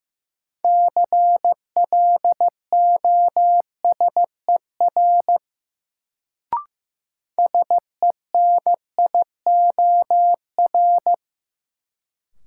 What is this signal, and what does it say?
Morse code, heard as a keyed tone, about 15 wpm, 700 Hz. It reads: CLOSER SENIOR
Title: Sets of 5 Words Encore - New Word Drill - 15wpm